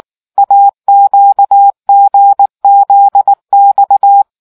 Morse code, heard as a keyed tone, about 19 wpm, 800 Hz. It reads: AQGZX